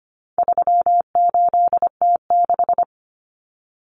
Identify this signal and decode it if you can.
Morse code, heard as a keyed tone, about 25 wpm, 700 Hz. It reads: 38T6
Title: Common Cut Numbers - 4-Digit 25wpm